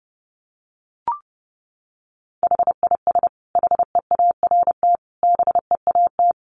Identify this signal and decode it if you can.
Morse code, heard as a keyed tone, about 30 wpm, 700 Hz. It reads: HIS HEART BEAT
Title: Sentences from Top 1300 Words 30wpm